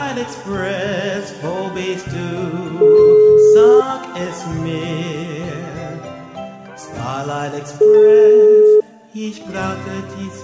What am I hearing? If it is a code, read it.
Ringback tone (Europe, 425 Hz).